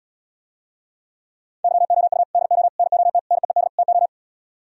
Morse code, heard as a keyed tone, about 77 wpm, 700 Hz. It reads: YOU DO NOT NEED TO